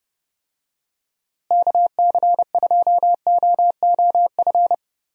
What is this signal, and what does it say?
Morse code, heard as a keyed tone, about 30 wpm, 700 Hz. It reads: KC2OOF
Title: USA Call Signs - 6 Characters - Repeated 3x - 30wpm